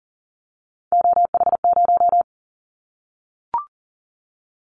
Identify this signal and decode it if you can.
Morse code, heard as a keyed tone, about 40 wpm, 690 Hz. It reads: OH0
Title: Callsign Prefixes for Top 100 DXCC - 40wpm